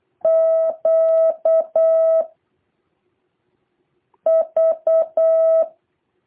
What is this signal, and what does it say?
Morse code, heard as a keyed tone, about 8 wpm, 650 Hz. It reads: Q V
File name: jh8cbh / 2023cw